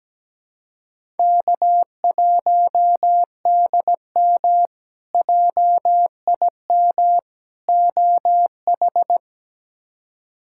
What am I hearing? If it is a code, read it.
Morse code, heard as a keyed tone, about 17 wpm, 700 Hz. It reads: K1DM JIM OH
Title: K1USN SST QSO Practice - 17wpm